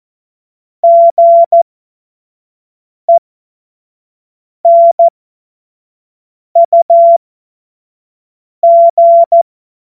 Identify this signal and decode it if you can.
Morse code, heard as a keyed tone, about 14 wpm, spaced as for 5 wpm, 680 Hz. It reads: GENUG
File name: Morsekurs / Witze / Witz 02. August 2022 25 F70